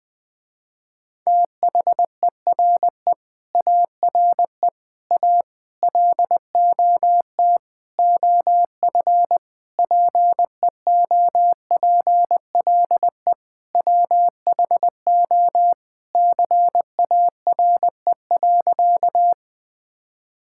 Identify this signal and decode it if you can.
Morse code, heard as a keyed tone, about 20 wpm, 700 Hz. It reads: THERE ARE A LOT OF PEOPLE WHO CARE.
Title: Sentences with 5-Word Phrases - Spoken Beginning - 20wpm